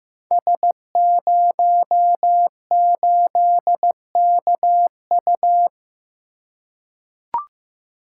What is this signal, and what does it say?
Morse code, heard as a keyed tone, about 15 wpm, 700 Hz. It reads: S08KU
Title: Code Groups - 5 Letter:Numbers - 15wpm